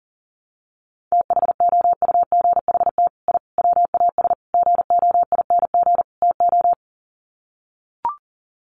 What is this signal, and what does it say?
Morse code, heard as a keyed tone, about 37 wpm, 700 Hz. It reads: THOUGHT I WAS GOING TO
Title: Sets of 5 Words Encore2 40wpm